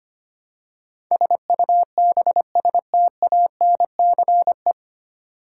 Morse code, heard as a keyed tone, about 25 wpm, 700 Hz. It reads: SUBSTANCE
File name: Nine Letter Words - 25wpm